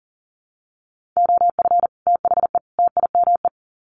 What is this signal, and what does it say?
Morse code, heard as a keyed tone, about 40 wpm, 700 Hz. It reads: OF THE TIME
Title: Sets of 3 Words 40wpm